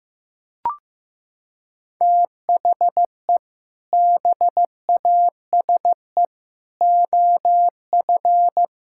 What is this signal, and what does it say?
Morse code, heard as a keyed tone, about 15 wpm, 700 Hz. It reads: THE BASE OF
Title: Sentences from Top 1000 Words - Review Part 5 - 15wpm